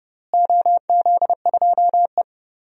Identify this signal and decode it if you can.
Morse code, heard as a keyed tone, about 30 wpm, 700 Hz. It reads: OZ2E
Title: DXCC-100 Call Signs - 4 Characters - Repeated 2x - 30wpm